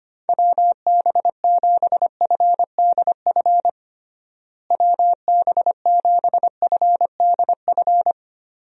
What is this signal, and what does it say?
Morse code, heard as a keyed tone, about 25 wpm, 700 Hz. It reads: WB7FDF WB7FDF
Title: USA Call Signs - 6 Characters - Repeated 2x - 25wpm